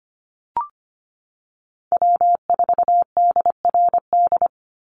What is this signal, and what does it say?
Morse code, heard as a keyed tone, about 25 wpm, 700 Hz. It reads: W4DRD
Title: USA Call Signs 25wpm